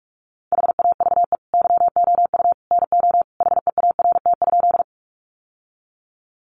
Morse code, heard as a keyed tone, about 44 wpm, 700 Hz. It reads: HAVE YOU NO HEART?
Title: Sentences from Top 1200 Words 45wpm